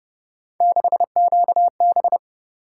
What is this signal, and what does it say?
Morse code, heard as a keyed tone, about 33 wpm, 700 Hz. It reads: 6QB